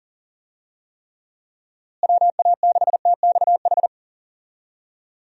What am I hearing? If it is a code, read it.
Morse code, heard as a keyed tone, about 40 wpm, 700 Hz. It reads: WA6TXH
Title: USA Call Signs - 6 Characters - Repeated 2x - 40wpm